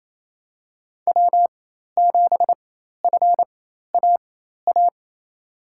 Morse code, heard as a keyed tone, about 27 wpm, spaced as for 13 wpm, 700 Hz. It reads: W7FAA
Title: USA Call Signs - 4x Character Spacing - 28wpm